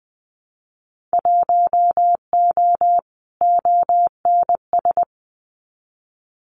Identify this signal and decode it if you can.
Morse code, heard as a keyed tone, about 20 wpm, 700 Hz. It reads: 1O ONS
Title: Winter Field Day Exchanges Encore 20wpm